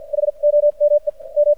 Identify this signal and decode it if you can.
Morse code, heard as a keyed tone, about 61 wpm, spaced as for 38 wpm, 590 Hz. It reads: HOMEM
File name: archived / CRSnet